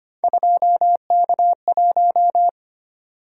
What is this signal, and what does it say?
Morse code, heard as a keyed tone, about 25 wpm, 700 Hz. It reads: 2K1
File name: Three Letter-Number Combinations - 25wpm